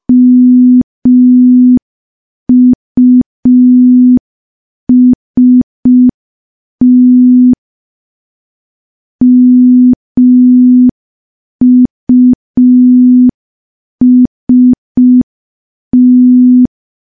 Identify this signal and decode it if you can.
Morse code, heard as a keyed tone, about 5 wpm, 250 Hz. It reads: MUST MUST
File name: morse / morse-training-05WPM